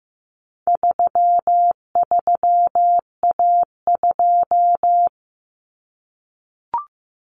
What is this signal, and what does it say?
Morse code, heard as a keyed tone, about 15 wpm, 700 Hz. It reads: 33A2
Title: Common Cut Numbers - 4-Digit 15wpm